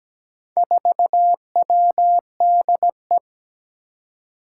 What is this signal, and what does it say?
Morse code, heard as a keyed tone, about 17 wpm, 700 Hz. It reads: 4WDE
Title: Four Challenging Letter-Number Combinations 17wpm